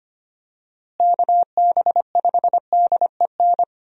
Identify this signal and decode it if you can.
Morse code, heard as a keyed tone, about 25 wpm, 700 Hz. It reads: KB5DEN
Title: USA Call Signs - 6 Characters - Repeated 2x - 25wpm